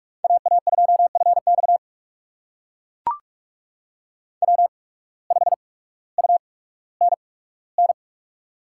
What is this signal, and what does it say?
Morse code, heard as a keyed tone, about 42 wpm, spaced as for 19 wpm, 700 Hz. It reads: AA2FX W5UNN
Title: USA Call Signs - 5 Characters - 8x Character Spacing - 45wpm